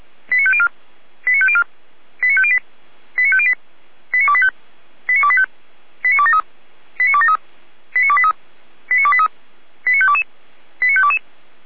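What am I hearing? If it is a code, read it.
Five-tone selective call (ZVEI), heard as quick runs of five tones, tones of about 70 ms. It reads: 89504 89504 89508 89508 89285 89285 89273 89273 89283 89283 89522 89522
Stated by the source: Five-tone ZVEI selective call received by SDR.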